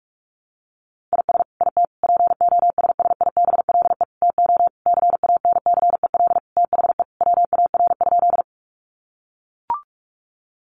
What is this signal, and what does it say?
Morse code, heard as a keyed tone, about 44 wpm, 700 Hz. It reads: IS IT POSSIBLE TO CANCEL THE WAR?
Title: Sentences from Top 1200 Words 45wpm